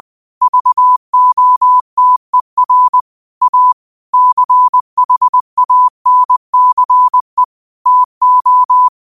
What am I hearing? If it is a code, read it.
Morse code, heard as a keyed tone, about 20 wpm, 1000 Hz. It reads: VOTER A CHANCE TO
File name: qotdfiles / 20wpm